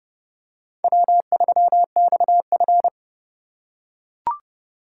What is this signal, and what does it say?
Morse code, heard as a keyed tone, about 29 wpm, 700 Hz. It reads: W3XF